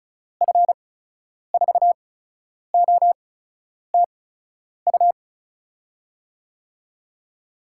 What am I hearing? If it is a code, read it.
Morse code, heard as a keyed tone, about 36 wpm, spaced as for 10 wpm, 700 Hz. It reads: F4OTU
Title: USA Call Signs - 6 Characters - 8x Character Spacing - 35wpm